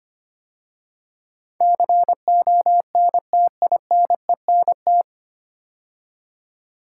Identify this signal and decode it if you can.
Morse code, heard as a keyed tone, about 25 wpm, 700 Hz. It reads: CONTINENT
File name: practice / 25wpm / 9 Letter Words - First 3 Letters Given - 25wpm